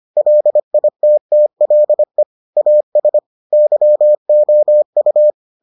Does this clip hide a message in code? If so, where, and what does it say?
Yes: Morse code from 0.2 to 5.3 s, heard as a keyed tone, about 25 wpm, 600 Hz. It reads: LITTLE AS YOU